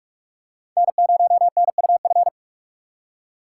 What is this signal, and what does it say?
Morse code, heard as a keyed tone, about 45 wpm, 700 Hz. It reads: N0NUF
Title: USA Call Signs - 4x Character Spacing - 45wpm